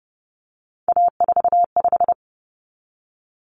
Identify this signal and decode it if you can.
Morse code, heard as a keyed tone, about 30 wpm, 700 Hz. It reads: A45
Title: Common Cut Numbers - 3-Digit 30wpm